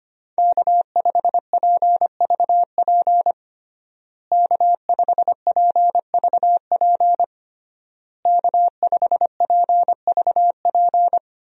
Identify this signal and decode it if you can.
Morse code, heard as a keyed tone, about 25 wpm, 700 Hz. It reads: K5PVP K5PVP K5PVP
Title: USA Call Signs - 5 Characters - Repeated 3x - 25wpm